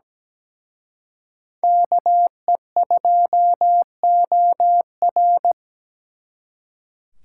Morse code, heard as a keyed tone, about 20 wpm, 700 Hz. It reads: KE2OR
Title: Speed Racing - USA 2x2 Call Signs 17wpm